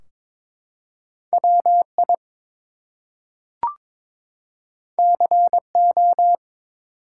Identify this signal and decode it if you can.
Morse code, heard as a keyed tone, about 22 wpm, 700 Hz. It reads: WI CO
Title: US States and Canadian Provinces 22wpm